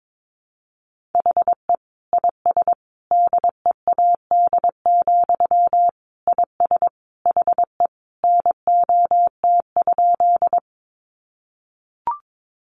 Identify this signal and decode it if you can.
Morse code, heard as a keyed tone, about 22 wpm, 700 Hz. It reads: HE IS DEAD, IS HE NOT?
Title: Sentences from Top 1500 Words 22wpm